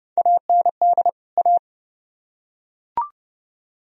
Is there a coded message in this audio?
Yes: Morse code from 0.2 to 1.6 s, heard as a keyed tone, about 28 wpm, 700 Hz. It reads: AND A